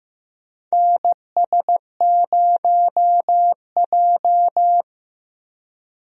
Morse code, heard as a keyed tone, about 15 wpm, 700 Hz. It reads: NS0J
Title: USA Call Signs - 4 Characters - Repeated 2x - 15wpm